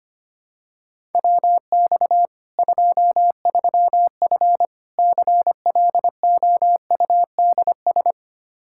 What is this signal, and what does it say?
Morse code, heard as a keyed tone, about 25 wpm, 700 Hz. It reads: WX 23F CLOUDS
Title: Practice QSO - Exchange 2 - Info - 25wpm